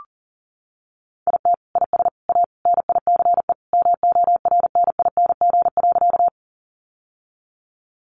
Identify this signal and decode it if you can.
Morse code, heard as a keyed tone, about 40 wpm, 700 Hz. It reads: IT IS A NICE MORNING.
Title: Sentences from Top 1100 Words 40wpm